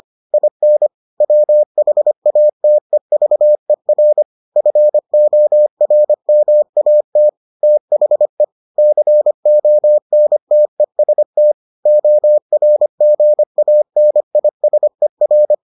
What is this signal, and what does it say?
Morse code, heard as a keyed tone, about 25 wpm, 600 Hz. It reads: IN WHATEVER FORMAT THE CONTEST ORGANISER